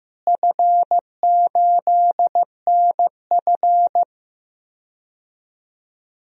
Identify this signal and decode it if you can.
Morse code, heard as a keyed tone, about 15 wpm, 700 Hz. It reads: F8NF
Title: DXCC-100 Call Signs - 4 Characters - Repeated 3x - 15wpm